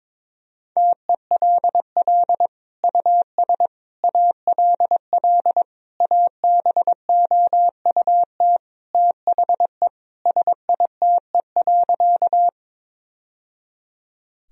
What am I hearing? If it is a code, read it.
Morse code, heard as a keyed tone, about 22 wpm, 700 Hz. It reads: TELL US ALL ABOUT THE SITE.
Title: Sentences from Top 1000 Words - Review Part 4 - 22wpm